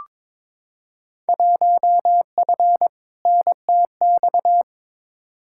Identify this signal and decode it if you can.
Morse code, heard as a keyed tone, about 22 wpm, 700 Hz. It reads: 1F NTX
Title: Field Day Exchanges Encore - 22wpm